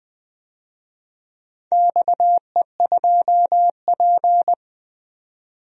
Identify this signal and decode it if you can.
Morse code, heard as a keyed tone, about 20 wpm, 700 Hz. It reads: XE2P
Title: DXCC-100 Call Signs - 4 Characters - 20wpm